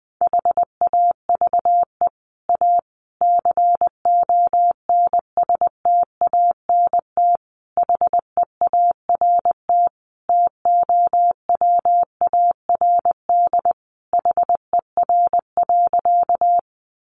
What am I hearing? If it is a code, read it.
Morse code, heard as a keyed tone, about 20 wpm, 700 Hz. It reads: HAVE A CONSTANT HEART TOWARD HER.
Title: Sentences from Top 1500 Words 20wpm